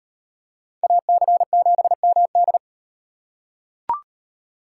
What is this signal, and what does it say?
Morse code, heard as a keyed tone, about 35 wpm, 700 Hz. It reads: AC7MD